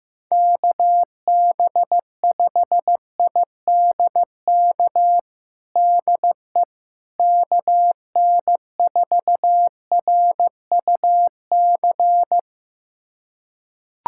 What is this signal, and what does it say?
Morse code, heard as a keyed tone, about 15 wpm, 700 Hz. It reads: KB5IDK DE KN4RUC
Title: USA Callsign DE Callsign - 6 Characters - 15wpm